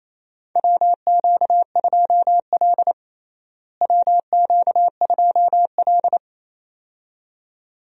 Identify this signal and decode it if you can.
Morse code, heard as a keyed tone, about 27 wpm, 700 Hz. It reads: WQ2L WQ2L
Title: USA Call Signs - 4 Characters - Repeated 2x - 28wpm